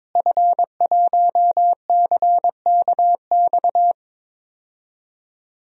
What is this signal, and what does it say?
Morse code, heard as a keyed tone, about 22 wpm, 700 Hz. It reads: F1CKX
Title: DXCC-100 Call Signs - 5 Characters - Repeated 2x - 22wpm